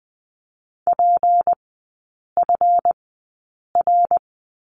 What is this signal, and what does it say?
Morse code, heard as a keyed tone, about 20 wpm, spaced as for 8 wpm, 700 Hz. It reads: PFR
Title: 191 Lesson 31e - Letter v - Three Characters 20wpm